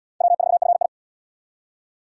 Morse code, heard as a keyed tone, about 70 wpm, 700 Hz. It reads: W4FI